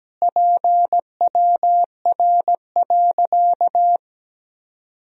Morse code, heard as a keyed tone, about 17 wpm, 700 Hz. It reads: PWR.
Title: Sentences from Top 500 Words and Abbreviations 17wpm